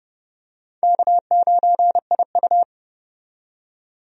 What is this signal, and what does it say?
Morse code, heard as a keyed tone, about 30 wpm, 700 Hz. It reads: K9IU
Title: USA Call Signs - 4 Characters - 30wpm